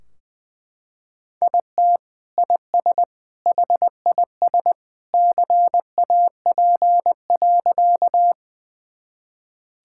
Morse code, heard as a keyed tone, about 20 wpm, 700 Hz. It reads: IT IS HIS CAP.
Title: Sentences from Top 1600 Words 20wpm